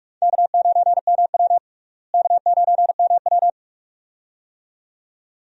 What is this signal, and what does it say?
Morse code, heard as a keyed tone, about 44 wpm, 700 Hz. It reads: K9MW K9MW
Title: USA Call Signs - 4 Characters - Repeated 2x - 45wpm